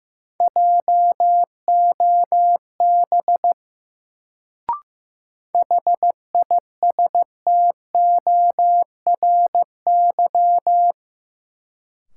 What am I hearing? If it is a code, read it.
Morse code, heard as a keyed tone, about 15 wpm, 700 Hz. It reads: JOB HISTORY
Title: Sentences from Top 600 Words - New Word Drill - 15wpm